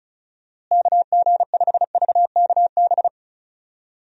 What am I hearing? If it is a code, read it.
Morse code, heard as a keyed tone, about 34 wpm, 700 Hz. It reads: KG5VKB
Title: USA Call Signs - 6 Characters - Repeated 3x - 35wpm